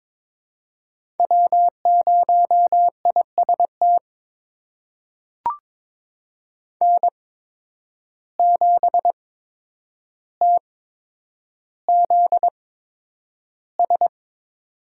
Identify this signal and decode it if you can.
Morse code, heard as a keyed tone, about 21 wpm, spaced as for 9 wpm, 700 Hz. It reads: W0IST N7TZS
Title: USA Call Signs - 5 Characters - 8x Character Spacing - 22wpm